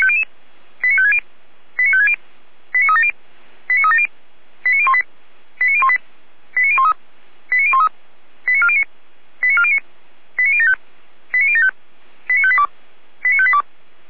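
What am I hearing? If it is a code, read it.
Five-tone selective call (ZVEI), heard as quick runs of five tones, tones of about 70 ms. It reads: 89560 89560 89370 89370 89018 89018 89013 89013 89409 89409 89075 89075 89582 89582
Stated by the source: ZVEI five-tone selective call received by SDR, recorded by Feuerwehr Orlamünde.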